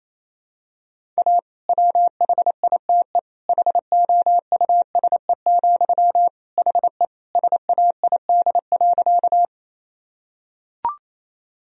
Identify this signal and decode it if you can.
Morse code, heard as a keyed tone, about 28 wpm, 700 Hz. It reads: A WHITE HOUSE, HE SAID.
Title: Sentences from Top 1000 Words - Review Part 1 - 28wpm